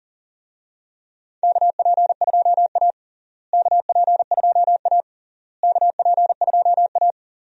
Morse code, heard as a keyed tone, about 40 wpm, 700 Hz. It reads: KP2A KP2A KP2A
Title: DXCC-100 Call Signs - 4 Characters - Repeated 3x - 40wpm